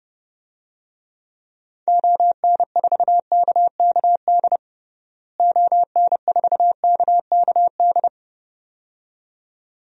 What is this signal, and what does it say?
Morse code, heard as a keyed tone, about 30 wpm, 700 Hz. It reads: ON4KKD ON4KKD
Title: DXCC-100 Call Signs - 6 Characters - Repeated 2x - 30wpm